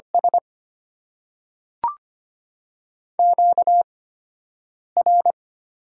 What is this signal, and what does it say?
Morse code, heard as a keyed tone, about 25 wpm, spaced as for 7 wpm, 700 Hz. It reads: S QR
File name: Top 90 QSO Elements - 8x Character Spacing - 25wpm